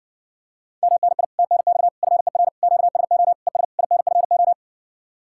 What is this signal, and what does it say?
Morse code, heard as a keyed tone, about 70 wpm, 700 Hz. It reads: MNI TNX FER QSO ES INFO